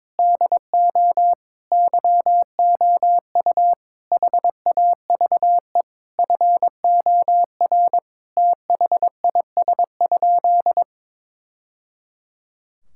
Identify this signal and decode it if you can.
Morse code, heard as a keyed tone, about 21 wpm, 700 Hz. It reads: DO YOU HAVE FOR THIS?
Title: Sentences from Top 1000 Words - Review Part 5 - 22wpm